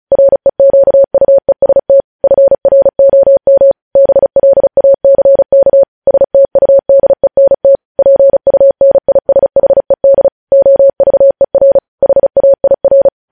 Morse code, heard as a keyed tone, about 36 wpm, 550 Hz. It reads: REQUEST FROM BLACK STUDENT PUNISHED OVER HAIR